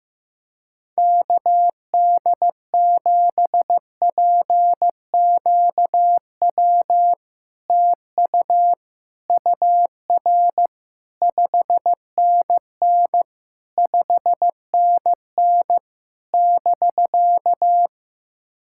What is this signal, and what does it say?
Morse code, heard as a keyed tone, about 15 wpm, 700 Hz. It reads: KD7PQW TU UR 5NN 5NN <BK>
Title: SOTA QSO Practice 15wpm